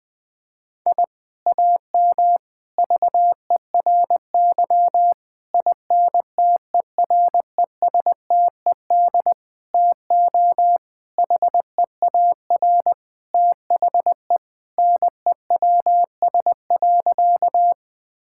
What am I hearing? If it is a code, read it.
Morse code, heard as a keyed tone, about 20 wpm, 700 Hz. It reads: I AM VERY INTERESTED TO HEAR THE NEWS.
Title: Sentences from Top 1000 Words - Review Part 6 - 20wpm